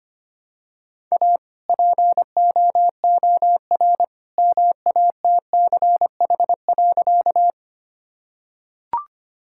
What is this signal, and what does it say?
Morse code, heard as a keyed tone, about 25 wpm, 700 Hz. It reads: A POOR MATCH.
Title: Sentences from Top 1600 Words 25wpm